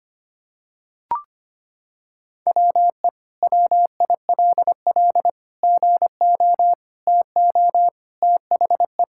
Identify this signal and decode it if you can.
Morse code, heard as a keyed tone, about 25 wpm, 700 Hz. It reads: WE WILL GO TO THE
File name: practice / 25wpm / Sentences with 2-Word Phrases - 25wpm